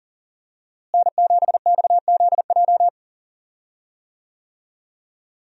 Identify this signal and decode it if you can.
Morse code, heard as a keyed tone, about 40 wpm, 700 Hz. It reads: N7XZJ
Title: USA Call Signs - 5 Characters - Repeated 3x - 40wpm